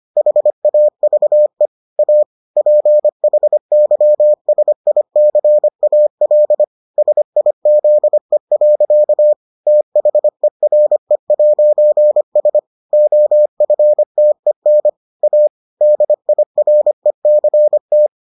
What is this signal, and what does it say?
Morse code, heard as a keyed tone, about 25 wpm, 600 Hz. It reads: HAVE A PHYSICAL SIZE. THERE'S OFTEN A DIRECT